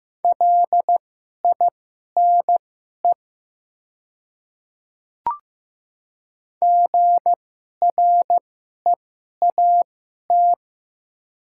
Morse code, heard as a keyed tone, about 15 wpm, spaced as for 11 wpm, 700 Hz. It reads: LINE GREAT
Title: 2x Character Spacing - Top 300 Words - 15wpm